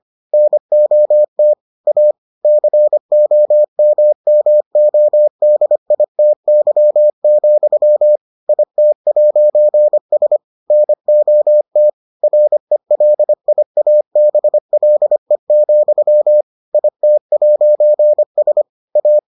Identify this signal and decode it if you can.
Morse code, heard as a keyed tone, about 25 wpm, 600 Hz. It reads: NOT A COMMODITY, IT'S NOT RELIABLE, IT'S A